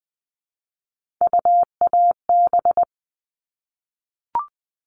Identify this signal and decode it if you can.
Morse code, heard as a keyed tone, about 20 wpm, 700 Hz. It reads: UAB
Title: All Cut Numbers - 3-Digit 20wpm